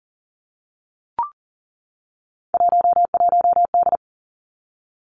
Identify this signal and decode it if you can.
Morse code, heard as a keyed tone, about 40 wpm, 700 Hz. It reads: 11D
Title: Three Challenging Letter-Number Combinations 40wpm